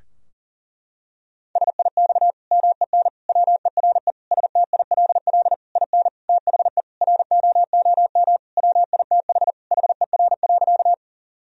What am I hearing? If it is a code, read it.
Morse code, heard as a keyed tone, about 40 wpm, 700 Hz. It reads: SIX MEN WERE STILL IN THE ROOM WITH HER.
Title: Sentences from Top 1000 Words - Review Part 2 - 40wpm